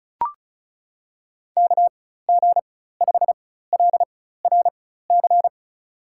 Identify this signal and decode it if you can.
Morse code, heard as a keyed tone, about 36 wpm, spaced as for 16 wpm, 700 Hz. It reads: KG5LRC
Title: USA Call Signs - 6 Characters - 4x Character Spacing - 35wpm